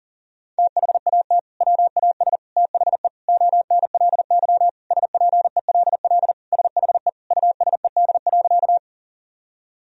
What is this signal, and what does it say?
Morse code, heard as a keyed tone, about 40 wpm, 700 Hz. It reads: THAT WAS THE ONLY SPELL SHE USED.
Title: Sentences from Top 1000 Words - Review Part 1 - 40wpm